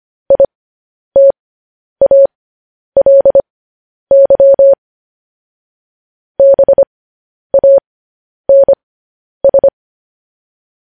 Morse code, heard as a keyed tone, about 25 wpm, spaced as for 10 wpm, 550 Hz. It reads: ITALY BANS
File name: podcast / News headlines-1x-25x10